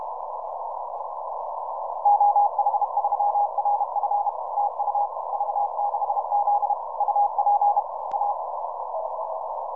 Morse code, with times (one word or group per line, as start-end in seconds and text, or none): 2.1-4.3 OH3RF
4.6-5.1 TU
5.3-6.7 V6Z
7.0-7.8 UP
8.1-8.3 A